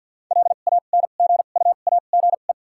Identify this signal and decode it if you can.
Morse code, heard as a keyed tone, about 53 wpm, 700 Hz. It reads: L A N G U A G E
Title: 2x Character Spacing - Top 500 Words - 50wpm